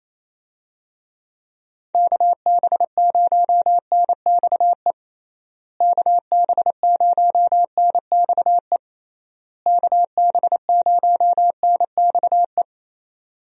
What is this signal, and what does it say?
Morse code, heard as a keyed tone, about 28 wpm, 700 Hz. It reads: KB0NXE KB0NXE KB0NXE
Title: USA Call Signs - 6 Characters - Repeated 3x - 28wpm